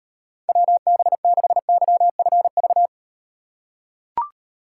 Morse code, heard as a keyed tone, about 38 wpm, 700 Hz. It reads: WB6YFV